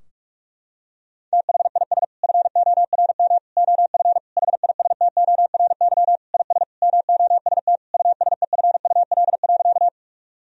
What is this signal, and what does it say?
Morse code, heard as a keyed tone, about 41 wpm, 700 Hz. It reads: THIS FORM OF HISTORY IS MOST USEFUL.
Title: Sentences from Top 1000 Words - Review Part 4 - 45wpm